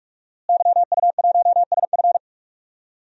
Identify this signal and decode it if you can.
Morse code, heard as a keyed tone, about 46 wpm, 700 Hz. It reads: YU1SF
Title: DXCC-100 Call Signs - 5 Characters - Repeated 2x - 45wpm